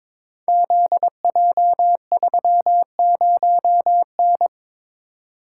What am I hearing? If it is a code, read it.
Morse code, heard as a keyed tone, about 23 wpm, 700 Hz. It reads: ZJ30N
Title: Code Groups - 5 Letter:Numbers - 22wpm